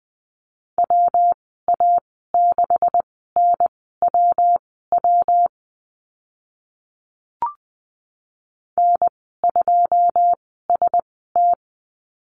Morse code, heard as a keyed tone, about 20 wpm, spaced as for 15 wpm, 700 Hz. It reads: WA6NWW N2ST
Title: 193 Lesson 31g - Letter v - Call Signs 20wpm